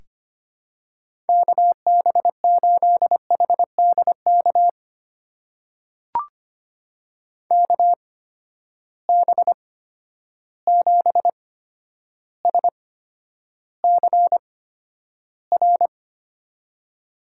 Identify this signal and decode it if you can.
Morse code, heard as a keyed tone, about 25 wpm, spaced as for 11 wpm, 700 Hz. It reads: KB8HDK KB7SCR
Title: USA Call Signs - 8x Character Spacing - 25wpm